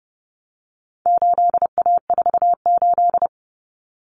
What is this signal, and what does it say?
Morse code, heard as a keyed tone, about 30 wpm, 700 Hz. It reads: 8A48
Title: Common Cut Numbers - 4-Digit 30wpm